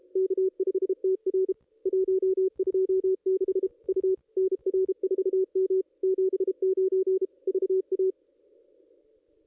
Morse code, heard as a keyed tone, about 37 wpm, 380 Hz. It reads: K5TR 126 U NR4M 79 VA
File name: contests / ss-cw-k5tr-op-n5rz -1900